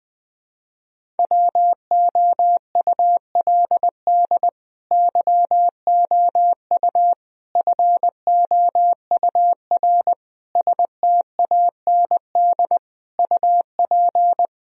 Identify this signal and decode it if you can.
Morse code, heard as a keyed tone, about 20 wpm, 700 Hz. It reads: WOULD YOU FOUR STAND UP
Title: Sentences from Top 1000 Words - Review Part 1 - 20wpm